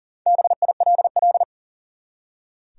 Morse code, heard as a keyed tone, about 40 wpm, 700 Hz. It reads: BILL
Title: USA Common Names - Max 5 Letters - 40wpm